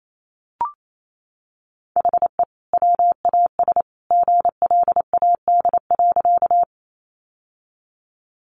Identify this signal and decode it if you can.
Morse code, heard as a keyed tone, about 28 wpm, 700 Hz. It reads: HE WAS GLAD.
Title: Sentences from Top 800 Words 28wpm